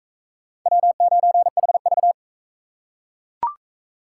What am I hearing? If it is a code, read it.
Morse code, heard as a keyed tone, about 41 wpm, 700 Hz. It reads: W9HV